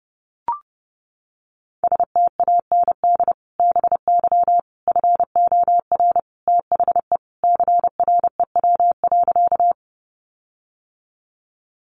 Morse code, heard as a keyed tone, about 30 wpm, 700 Hz. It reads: STAND BY FOR THE CREW.
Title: Sentences from Top 1500 Words 30wpm